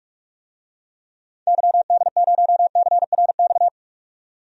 Morse code, heard as a keyed tone, about 42 wpm, 700 Hz. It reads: YD0CRX
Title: DXCC-100 Call Signs - 6 Characters - 45wpm